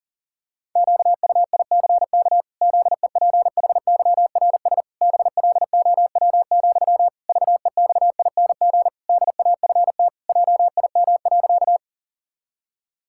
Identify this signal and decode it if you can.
Morse code, heard as a keyed tone, about 40 wpm, 700 Hz. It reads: QUICK ZEPHYRS BLOW, VEXING DAFT JIM.
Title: Pangram Challenge v2 - 40wpm